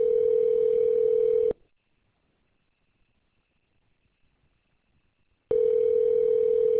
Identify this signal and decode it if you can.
Ringback tone (North America), 440 + 480 Hz.